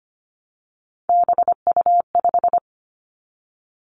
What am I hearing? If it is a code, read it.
Morse code, heard as a keyed tone, about 25 wpm, 700 Hz. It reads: BU5
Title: Three Challenging Letter-Number Combinations 25wpm